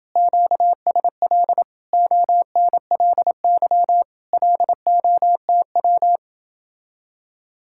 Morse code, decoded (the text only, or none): QSL ONLY LOTW